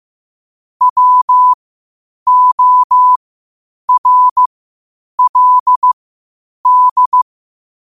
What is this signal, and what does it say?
Morse code, heard as a keyed tone, about 15 wpm, spaced as for 8 wpm, 1000 Hz. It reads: WORLD
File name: qotdfiles / 8wpm